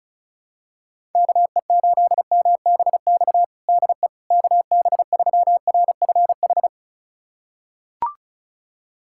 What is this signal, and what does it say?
Morse code, heard as a keyed tone, about 36 wpm, 700 Hz. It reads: KE8MBX DE KB3RFH
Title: USA Callsign DE Callsign 35wpm